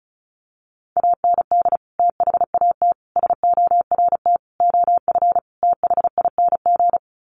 Morse code, heard as a keyed tone, about 34 wpm, 700 Hz. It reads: AND THAT SORT OF THING
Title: Sets of 5 Words Encore2 35wpm